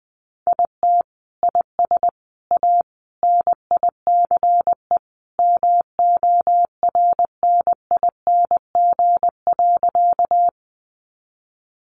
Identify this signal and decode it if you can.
Morse code, heard as a keyed tone, about 20 wpm, 700 Hz. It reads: IT IS A NICE MORNING.
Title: Sentences from Top 1100 Words 20wpm